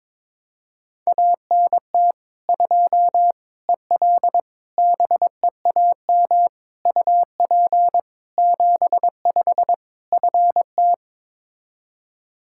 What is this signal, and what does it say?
Morse code, heard as a keyed tone, about 23 wpm, 700 Hz. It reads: ANT 2 EL BEAM UP 75 FT
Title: Practice QSO - Exchange 2 - Info - 22wpm